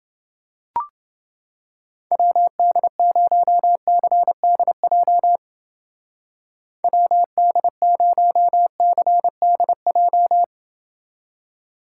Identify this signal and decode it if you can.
Morse code, heard as a keyed tone, about 28 wpm, 700 Hz. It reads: WD0CDJ WD0CDJ